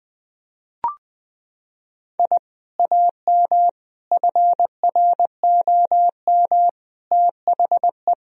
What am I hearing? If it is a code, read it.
Morse code, heard as a keyed tone, about 21 wpm, 700 Hz. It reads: I AM FROM THE
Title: Sentences from Top 1000 Words - Review Part 1 - 20wpm